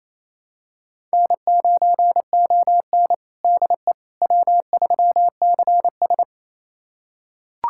Morse code, heard as a keyed tone, about 28 wpm, 700 Hz. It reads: N9ON DE W3CS
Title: USA Callsign DE Callsign - 4 Characters - 28wpm